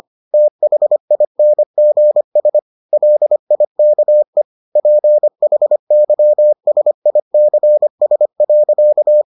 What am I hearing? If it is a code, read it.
Morse code, heard as a keyed tone, about 25 wpm, 600 Hz. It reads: THINGS LIKE PHYSICS.